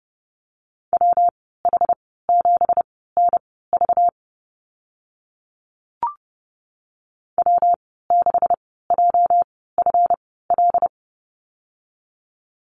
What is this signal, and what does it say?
Morse code, heard as a keyed tone, about 30 wpm, spaced as for 17 wpm, 700 Hz. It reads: WH7NV W6JFL
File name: Lesson 37g - Letter x - Call Signs 30wpm